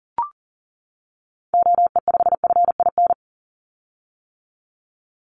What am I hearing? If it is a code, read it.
Morse code, heard as a keyed tone, about 41 wpm, 700 Hz. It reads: OE5FIN
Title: Call Signs - 40wpm